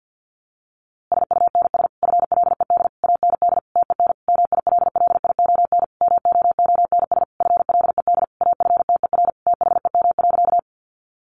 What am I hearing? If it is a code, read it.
Morse code, heard as a keyed tone, about 50 wpm, 700 Hz. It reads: SUNS FLED AND TEN MILLION MOONS FLED AFTER THEM.
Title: A Sound Of Thunder 50wpm